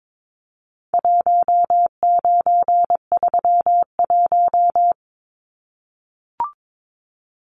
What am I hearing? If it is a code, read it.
Morse code, heard as a keyed tone, about 22 wpm, 700 Hz. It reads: 1931
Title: Numbers - 4-Digit 22wpm